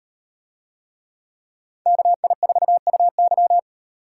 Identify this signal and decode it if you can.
Morse code, heard as a keyed tone, about 38 wpm, 700 Hz. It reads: KI4UY